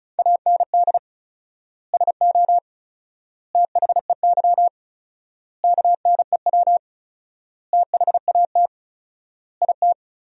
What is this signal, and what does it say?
Morse code, heard as a keyed tone, about 34 wpm, 700 Hz. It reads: AND SO THEY KNEW THAT IT
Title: 4x Word-Spacing - Sentences from Top 400 Words - 35wpm